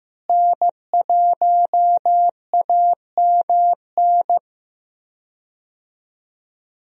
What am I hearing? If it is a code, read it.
Morse code, heard as a keyed tone, about 15 wpm, 700 Hz. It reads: N1AMN
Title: USA Call Signs - 5 Characters - Repeated 3x - 15wpm